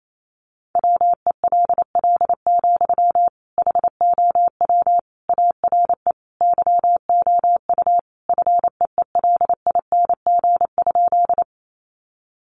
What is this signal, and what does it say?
Morse code, heard as a keyed tone, about 28 wpm, 700 Hz. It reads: WELL, HOW ARE YOU FEELING?
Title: Sentences from Top 1200 Words 28wpm